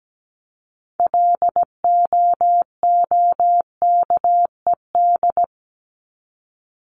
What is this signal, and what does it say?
Morse code, heard as a keyed tone, about 17 wpm, 700 Hz. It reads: LOOKED
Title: ED Suffix - Top 100 Words with Base Words 17wpm